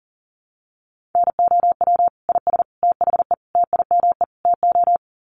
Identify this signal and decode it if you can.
Morse code, heard as a keyed tone, about 40 wpm, 700 Hz. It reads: NOW IS THE TIME TO
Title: Sets of 5 Words Encore2 40wpm